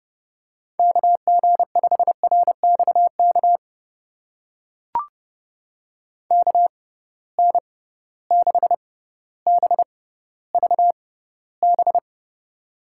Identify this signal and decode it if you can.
Morse code, heard as a keyed tone, about 30 wpm, spaced as for 15 wpm, 700 Hz. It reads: KG5RXK KN6BVB
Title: USA Call Signs - 6x Character Spacing - 30wpm